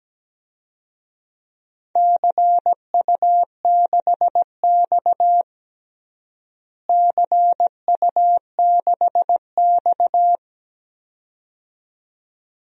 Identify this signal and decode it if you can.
Morse code, heard as a keyed tone, about 17 wpm, 700 Hz. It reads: CU6X CU6X
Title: DXCC-100 Call Signs - 4 Characters - Repeated 2x - 17wpm